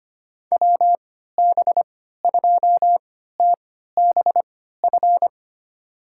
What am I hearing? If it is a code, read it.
Morse code, heard as a keyed tone, about 25 wpm, spaced as for 14 wpm, 700 Hz. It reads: WB2TBF
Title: USA Call Signs - 3x Character Spacing - 25wpm